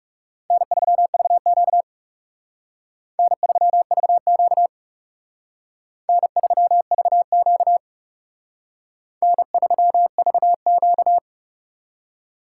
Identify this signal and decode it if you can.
Morse code, heard as a keyed tone, about 36 wpm, 700 Hz. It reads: N3VQ N3VQ N3VQ N3VQ
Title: Speed Racing - USA 1x2 Call Signs 30wpm